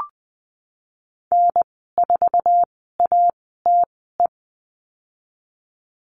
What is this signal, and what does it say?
Morse code, heard as a keyed tone, about 19 wpm, spaced as for 14 wpm, 700 Hz. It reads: N4ATE
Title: 167 Lesson 27g - Number 7 - Call Signs 20wpm